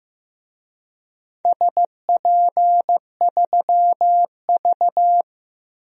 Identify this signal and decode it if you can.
Morse code, heard as a keyed tone, about 15 wpm, 700 Hz. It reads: SP3V
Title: DXCC-100 Call Signs - 4 Characters - Repeated 3x - 15wpm